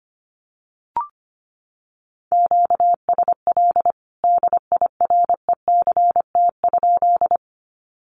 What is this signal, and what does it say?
Morse code, heard as a keyed tone, about 25 wpm, 700 Hz. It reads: QSL DIRECT?
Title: Common QSO Phrases 25wpm